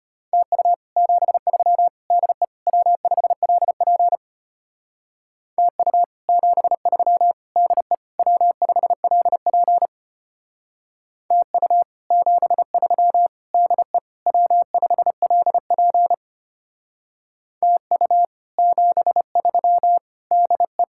Morse code, heard as a keyed tone, about 31 wpm, 700 Hz. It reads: TU 73 DE W5LP TU 73 DE W5LP TU 73 DE W5LP TU 73 DE
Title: Speed-Racing Common QSO Phrases - 25wpm